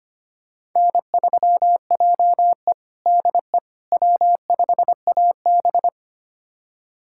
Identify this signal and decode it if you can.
Morse code, heard as a keyed tone, about 25 wpm, 700 Hz. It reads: N3JE DE W5AB
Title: USA Callsign DE Callsign - 4 Characters - 25wpm